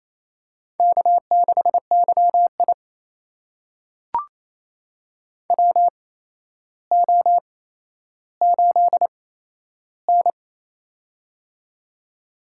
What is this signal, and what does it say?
Morse code, heard as a keyed tone, about 28 wpm, spaced as for 12 wpm, 700 Hz. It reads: K6YI WO8N
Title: USA Call Signs - 4 Characters - 8x Character Spacing - 28wpm